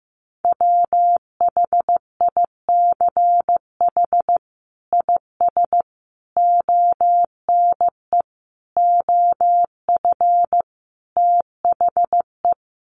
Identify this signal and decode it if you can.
Morse code, heard as a keyed tone, about 15 wpm, 700 Hz. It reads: WHICH IS ONE OF THE
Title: Sets of 5 Words Encore 15wpm